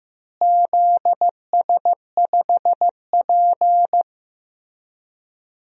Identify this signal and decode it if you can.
Morse code, heard as a keyed tone, about 15 wpm, 700 Hz. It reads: ZS5P
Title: DXCC-100 Call Signs - 4 Characters - Repeated 2x - 15wpm